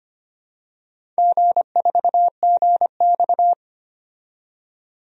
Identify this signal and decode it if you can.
Morse code, heard as a keyed tone, about 26 wpm, 690 Hz. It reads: G4GX
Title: DXCC-100 Call Signs - 4 Characters - 25wpm